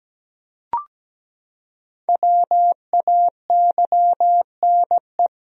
Morse code, heard as a keyed tone, about 17 wpm, 700 Hz. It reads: WAYNE
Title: USA Common Names - Max 5 Letters - 17wpm